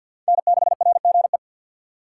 Morse code, heard as a keyed tone, about 46 wpm, 700 Hz. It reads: N6RGE
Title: USA Call Signs - 5 Characters - 3x Character Spacing - 50wpm